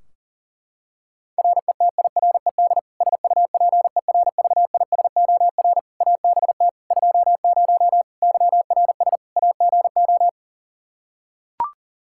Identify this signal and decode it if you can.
Morse code, heard as a keyed tone, about 40 wpm, 700 Hz. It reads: RETIRED SUPERVISOR ABT 20 YRS AGO
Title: Practice QSO - Exchange 3 - More Info - 40wpm